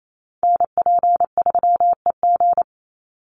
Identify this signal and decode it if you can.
Morse code, heard as a keyed tone, about 28 wpm, 700 Hz. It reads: NP3EG
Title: USA Call Signs 28wpm